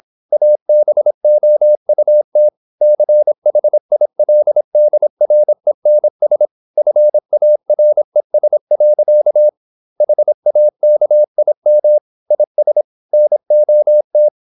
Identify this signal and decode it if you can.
Morse code, heard as a keyed tone, about 26 wpm, 600 Hz. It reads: ABOUT CHILDRENS FARES. HAKIM IS NOT
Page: Never 26wpm | CW med Gnister